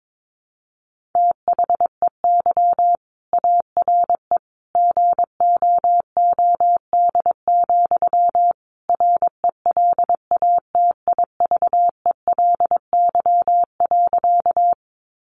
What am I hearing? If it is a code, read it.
Morse code, heard as a keyed tone, about 22 wpm, 700 Hz. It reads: THEY ARE GOOD, RELATIVELY.
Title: Sentences from Top 1100 Words 22wpm